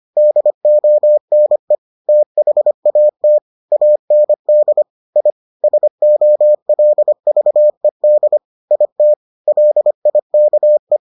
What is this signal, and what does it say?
Morse code, heard as a keyed tone, about 25 wpm, 600 Hz. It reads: DONE THAT AND I SOLVED IT LIKE